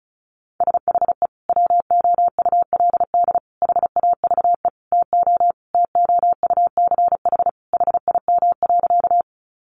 Morse code, heard as a keyed tone, about 34 wpm, 700 Hz. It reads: SHE WOULD HAVE TO TOUCH HIM.
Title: Sentences from Top 1200 Words 35wpm